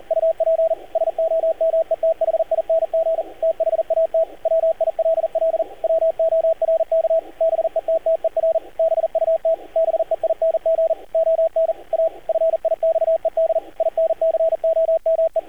Morse code, heard as a keyed tone, about 38 wpm, 650 Hz, with radio noise. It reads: UP SOMETHING THAT WILL WORK BETTER BUT BEING ON A FIXED INCOME
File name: archived / CRSnet